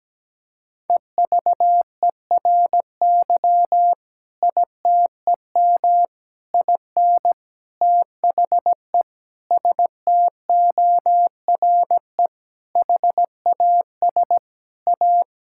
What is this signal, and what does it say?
Morse code, heard as a keyed tone, about 17 wpm, 700 Hz. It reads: EVERY ITEM IN THE STORE HAS A
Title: Sentences from Top 1000 Words - Review Part 6 - 17wpm